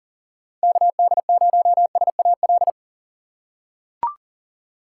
Morse code, heard as a keyed tone, about 38 wpm, 700 Hz. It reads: KD0SAL
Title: USA Call Signs - 6 Characters - 2x Character Spacing - 40wpm